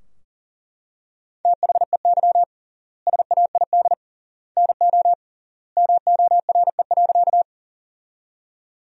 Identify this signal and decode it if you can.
Morse code, heard as a keyed tone, about 40 wpm, 700 Hz. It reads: THEY SAID NO MORE.
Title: 3x Word-Spacing - Sentences from Top 100 Words - 40wpm